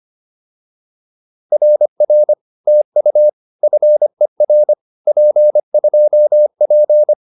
Morse code, heard as a keyed tone, about 25 wpm, 600 Hz. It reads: RR TU FER P2P
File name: POTA306a 25